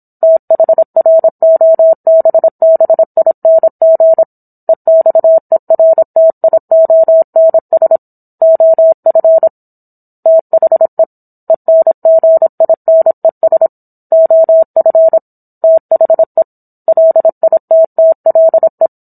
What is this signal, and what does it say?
Morse code, heard as a keyed tone, about 27 wpm, 650 Hz. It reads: THROBBING EXERTIONS OF THE ENGINES OF THE LITTLE
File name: Book One / 17-Chapter 17 - 26 WPM